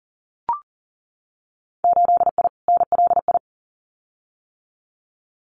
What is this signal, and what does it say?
Morse code, heard as a keyed tone, about 40 wpm, 700 Hz. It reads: 8I NLI
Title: Winter Field Day Exchanges Encore 40wpm